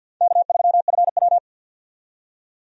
Morse code, heard as a keyed tone, about 52 wpm, 700 Hz. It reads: K3FW
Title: USA Call Signs - 4 Characters - 50wpm